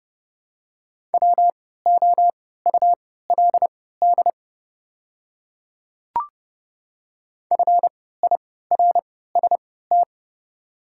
Morse code, heard as a keyed tone, about 29 wpm, spaced as for 17 wpm, 700 Hz. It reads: WOULD FIRST
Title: 3x Character Spacing - Top 200 Words - 30wpm